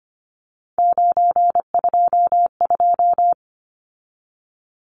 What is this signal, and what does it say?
Morse code, heard as a keyed tone, about 25 wpm, 700 Hz. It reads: 922
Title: Numbers - 3-Digit 25wpm